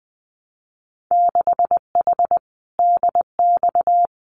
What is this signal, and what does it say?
Morse code, heard as a keyed tone, about 20 wpm, 700 Hz. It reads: 6H DX
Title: Winter Field Day Exchanges Encore 20wpm